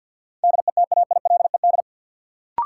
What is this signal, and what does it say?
Morse code, heard as a keyed tone, about 53 wpm, 700 Hz. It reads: DETAILED
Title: Sentences from Top 1500 Words - New Word Drill - 50wpm